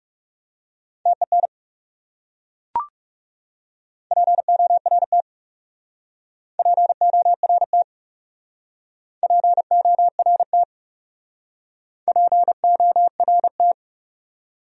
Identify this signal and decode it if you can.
Morse code, heard as a keyed tone, about 40 wpm, 700 Hz. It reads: TEN PORT PORT PORT PORT